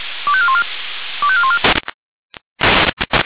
Five-tone selective call (ZVEI), heard as quick runs of five tones, tones of about 70 ms. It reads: 25414 25414
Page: SDR/ZVEI